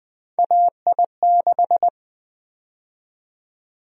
Morse code, heard as a keyed tone, about 20 wpm, 700 Hz. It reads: AI6
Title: USA Call Sign Prefixes - 20wpm